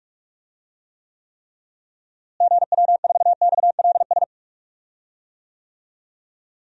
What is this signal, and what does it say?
Morse code, heard as a keyed tone, about 41 wpm, 700 Hz. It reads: GW4XLS
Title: DXCC-100 Call Signs - 6 Characters - Repeated 3x - 45wpm